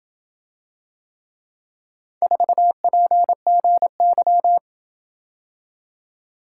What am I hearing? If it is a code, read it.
Morse code, heard as a keyed tone, about 27 wpm, 700 Hz. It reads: 4PGY